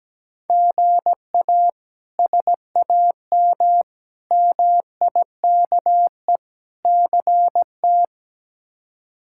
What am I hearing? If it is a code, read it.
Morse code, heard as a keyed tone, about 17 wpm, 700 Hz. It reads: GA SAM MIKE CT
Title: K1USN SST QSO Practice - 17wpm